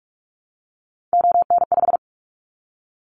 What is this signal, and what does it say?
Morse code, heard as a keyed tone, about 44 wpm, 700 Hz. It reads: ON5
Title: Callsign Prefixes for Top 100 DXCC - 45wpm